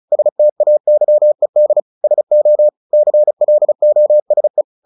Morse code, heard as a keyed tone, about 34 wpm, 600 Hz. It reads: STAYED SO CLOSE
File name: the old man and the sea / the old man and the sea 1 35wpm